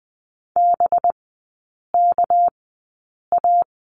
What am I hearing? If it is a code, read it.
Morse code, heard as a keyed tone, about 20 wpm, spaced as for 8 wpm, 700 Hz. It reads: BKA
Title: 212 Lesson 34e - Number 8 - Three Characters 20wpm